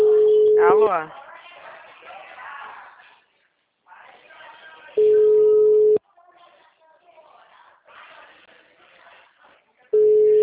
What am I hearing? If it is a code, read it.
Ringback tone (Europe, 425 Hz), with speech.